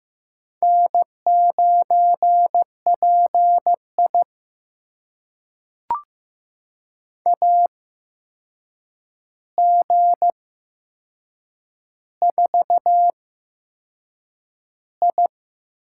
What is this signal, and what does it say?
Morse code, heard as a keyed tone, about 15 wpm, spaced as for 6 wpm, 700 Hz. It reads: N9PI AG4I
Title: USA Call Signs - 4 Characters - 8x Character Spacing - 15wpm